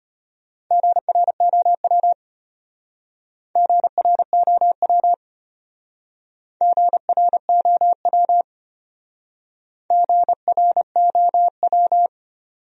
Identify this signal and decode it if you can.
Morse code, heard as a keyed tone, about 32 wpm, 700 Hz. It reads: GROW GROW GROW GROW